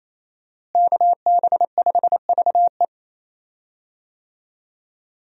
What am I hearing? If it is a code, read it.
Morse code, heard as a keyed tone, about 28 wpm, 700 Hz. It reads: KB5VE
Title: USA Call Signs - 5 Characters - Repeated 3x - 28wpm